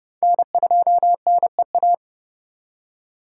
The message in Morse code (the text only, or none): N2NEA